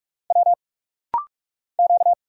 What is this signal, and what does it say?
Morse code, heard as a keyed tone, about 44 wpm, 700 Hz. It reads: W Q
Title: Single Letters - Rapid-Fire 45wpm